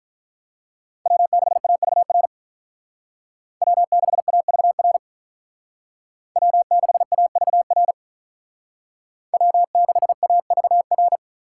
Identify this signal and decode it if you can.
Morse code, heard as a keyed tone, about 46 wpm, 700 Hz. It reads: W6AVR W6AVR W6AVR W6AVR